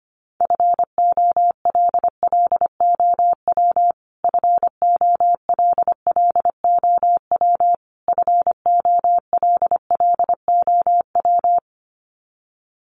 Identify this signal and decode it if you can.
Morse code, heard as a keyed tone, about 25 wpm, 700 Hz. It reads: FOLLOW FOLLOW FOLLOW
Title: Top 300 Words Repeated 25wpm